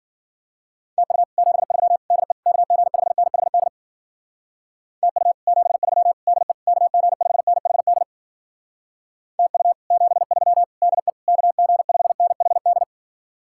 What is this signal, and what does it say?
Morse code, heard as a keyed tone, about 52 wpm, 700 Hz. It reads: TU 73 DE KG5NHD TU 73 DE KG5NHD TU 73 DE KG5NHD